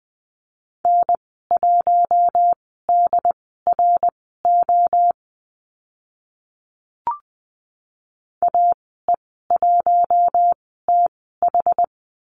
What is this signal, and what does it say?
Morse code, heard as a keyed tone, about 20 wpm, spaced as for 14 wpm, 700 Hz. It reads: N1DRO AE1TH
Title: 076 Lesson 13g - Letter L - Call Signs 20wpm